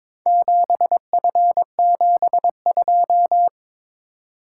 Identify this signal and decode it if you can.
Morse code, heard as a keyed tone, about 21 wpm, 700 Hz. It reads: 7F72
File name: practice / 22wpm / Four Letter-Number Combinations - 22wpm